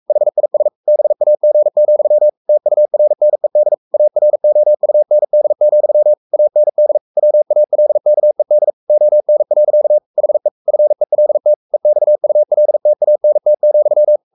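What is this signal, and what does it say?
Morse code, heard as a keyed tone, about 44 wpm, 600 Hz. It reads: HIS BAG, TURNED AROUND, AND WALKED ON. HE FELT EXULTANT,
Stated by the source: Morse code at 43 wpm sending Ken Follett's novel Never in English, from CW med Gnister.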